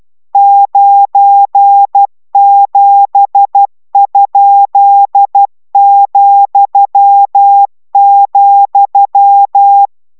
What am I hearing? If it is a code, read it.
Morse code, heard as a keyed tone, about 12 wpm, 800 Hz. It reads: 97?,,